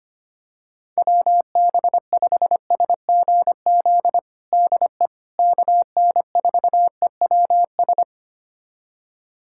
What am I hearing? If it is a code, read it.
Morse code, heard as a keyed tone, about 25 wpm, 700 Hz. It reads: WB5SGZ DE KN4EWS
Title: USA Callsign DE Callsign 25wpm